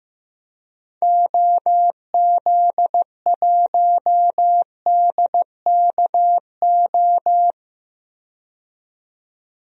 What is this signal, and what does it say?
Morse code, heard as a keyed tone, about 15 wpm, 700 Hz. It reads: OZ1DKO
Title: DXCC-100 Call Signs - 6 Characters - 15wpm